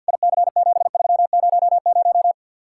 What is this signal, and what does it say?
Morse code, heard as a keyed tone, about 61 wpm, 700 Hz. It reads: IC7300
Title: Speed-Racing Common QSO Phrases - 50wpm